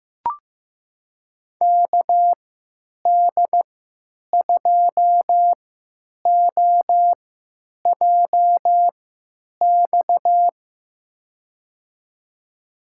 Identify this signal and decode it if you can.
Morse code, heard as a keyed tone, about 15 wpm, spaced as for 9 wpm, 700 Hz. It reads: KD2OJX
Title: USA Call Signs - 6 Characters - 3x Character Spacing - 15wpm